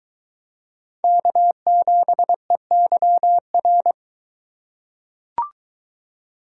Morse code, heard as a keyed tone, about 23 wpm, 700 Hz. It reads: K7EYR